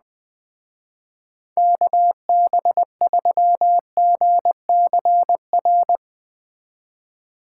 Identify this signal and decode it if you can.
Morse code, heard as a keyed tone, about 21 wpm, 700 Hz. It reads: KB3GCR